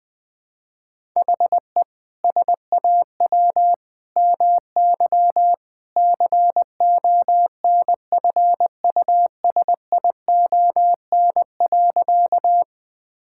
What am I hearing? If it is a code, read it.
Morse code, heard as a keyed tone, about 21 wpm, 700 Hz. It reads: HE SAW MY CONFUSION.
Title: Sentences from Top 1600 Words 20wpm